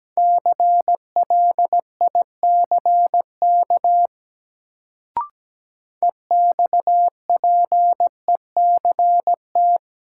Morse code, heard as a keyed tone, about 17 wpm, 700 Hz. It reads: CLICK EXPECT
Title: Sentences from Top 1100 Words - New Word Drill - 17wpm